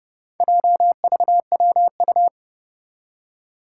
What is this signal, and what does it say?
Morse code, heard as a keyed tone, about 30 wpm, 700 Hz. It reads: JVWU
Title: Four Challenging Letter-Number Combinations 30wpm